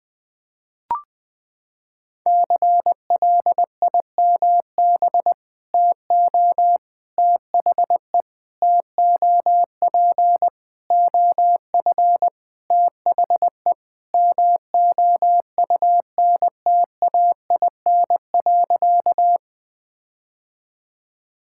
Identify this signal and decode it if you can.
Morse code, heard as a keyed tone, about 21 wpm, 700 Hz. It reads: CLIMB TO THE TOP OF THE MOUNTAIN.
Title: Sentences with 5-Word Phrases - 20wpm